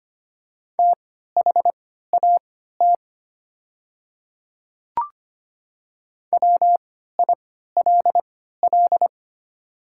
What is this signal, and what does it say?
Morse code, heard as a keyed tone, about 25 wpm, spaced as for 14 wpm, 700 Hz. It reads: THAT WILL
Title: 3x Character Spacing - Top 100 Words - 25wpm